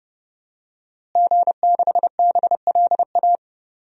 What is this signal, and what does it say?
Morse code, heard as a keyed tone, about 30 wpm, 700 Hz. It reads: G6BLA
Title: DXCC-100 Call Signs - 5 Characters - Repeated 2x - 30wpm